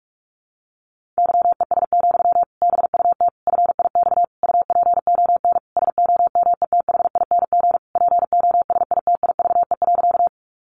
Morse code, heard as a keyed tone, about 44 wpm, 700 Hz. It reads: YES, BUT FIX UPON SOMETHING POSITIVE.
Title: Sentences from Top 1400 Words 45wpm